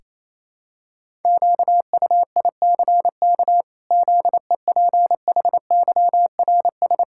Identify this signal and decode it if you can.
Morse code, heard as a keyed tone, about 28 wpm, 700 Hz. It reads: QUICK ZEPHYRS
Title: Pangram Challenge v2 - 28wpm